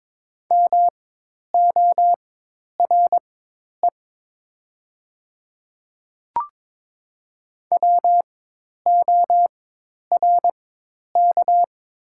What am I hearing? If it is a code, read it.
Morse code, heard as a keyed tone, about 22 wpm, spaced as for 10 wpm, 700 Hz. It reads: MORE WORK
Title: 4x Character Spacing - Top 200 Words - 22wpm